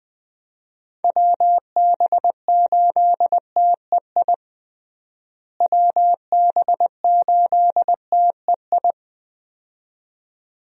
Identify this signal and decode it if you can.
Morse code, heard as a keyed tone, about 21 wpm, 700 Hz. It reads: WB8TEI WB8TEI
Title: USA Call Signs - 6 Characters - Repeated 2x - 20wpm